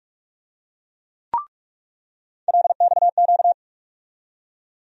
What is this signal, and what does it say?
Morse code, heard as a keyed tone, about 46 wpm, 700 Hz. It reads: LXQ
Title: USA Call Sign Postfixes - 3 Letters 45wpm